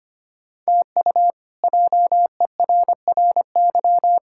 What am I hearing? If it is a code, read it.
Morse code, heard as a keyed tone, about 25 wpm, 700 Hz. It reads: TU JERRY
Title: K1USN SST QSO Practice - 25wpm